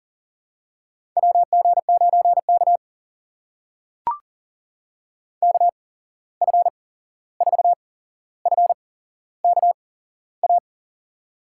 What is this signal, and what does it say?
Morse code, heard as a keyed tone, about 38 wpm, spaced as for 15 wpm, 700 Hz. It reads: WG9K KF4FKA
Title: USA Call Signs - 8x Character Spacing - 40wpm